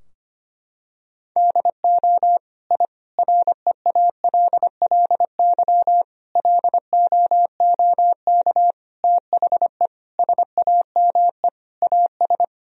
Morse code, heard as a keyed tone, about 25 wpm, 700 Hz. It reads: DO I REALLY LOOK THE SAME AS
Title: Sentences from Top 1000 Words - Review Part 2 - 25wpm